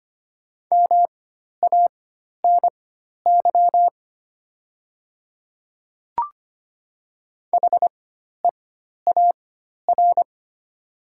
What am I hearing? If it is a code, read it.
Morse code, heard as a keyed tone, about 25 wpm, spaced as for 12 wpm, 700 Hz. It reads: MANY HEAR
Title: 4x Character Spacing - Top 300 Words - 25wpm